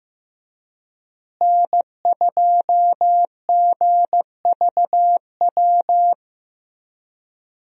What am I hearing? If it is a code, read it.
Morse code, heard as a keyed tone, about 16 wpm, 700 Hz. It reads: N2GVW
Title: USA Call Signs - 5 Characters - Repeated 3x - 15wpm